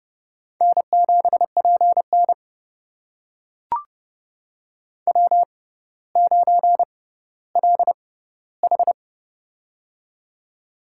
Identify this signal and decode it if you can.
Morse code, heard as a keyed tone, about 30 wpm, spaced as for 15 wpm, 700 Hz. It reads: N7PN W9LH
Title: USA Call Signs - 4 Characters - 6x Character Spacing - 30wpm